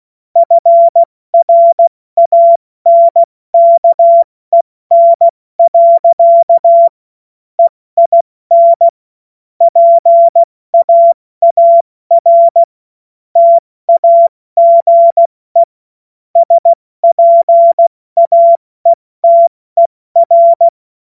Morse code, heard as a keyed tone, about 16 wpm, 680 Hz. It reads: FRANKEN. EIN PAAR TAGE SPAETER
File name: Morsekurs / Witze / Witz 04. Oktober 2022 70 F80